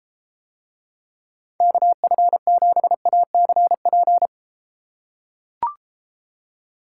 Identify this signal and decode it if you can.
Morse code, heard as a keyed tone, about 31 wpm, 700 Hz. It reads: KF7ACP